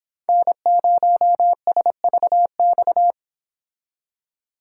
Morse code, heard as a keyed tone, about 26 wpm, 700 Hz. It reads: N0SVX